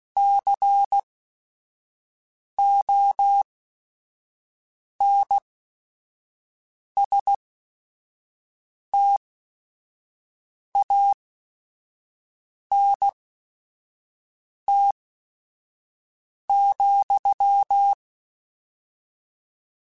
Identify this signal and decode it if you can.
Morse code, heard as a keyed tone, about 16 wpm, spaced as for 5 wpm, 800 Hz. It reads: CONSTANT,
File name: Technical Subjects / Technical Subjects-mp3s / XL @5